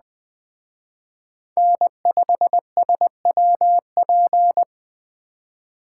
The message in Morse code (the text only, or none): N5SWP